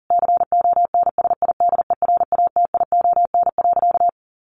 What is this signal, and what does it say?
Morse code, heard as a keyed tone, about 40 wpm, 700 Hz. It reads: CONSIDERATION.
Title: Sentences from Top 1500 Words 40wpm